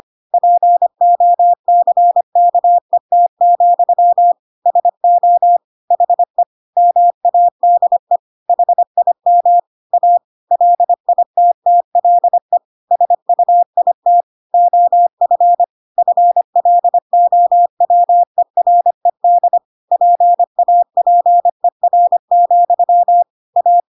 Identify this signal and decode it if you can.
Morse code, heard as a keyed tone, about 25 wpm, 700 Hz. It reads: POCKET, SO HE MADE HIM A LITTLE SUIT OF FLOWERED PAPER, A